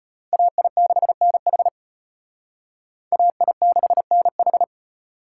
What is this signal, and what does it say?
Morse code, heard as a keyed tone, about 32 wpm, 700 Hz. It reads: AI6NH AI6NH